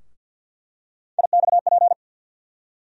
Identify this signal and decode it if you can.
Morse code, heard as a keyed tone, about 48 wpm, 700 Hz. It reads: IXP
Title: USA Call Sign Postfixes - 3 Letters 50wpm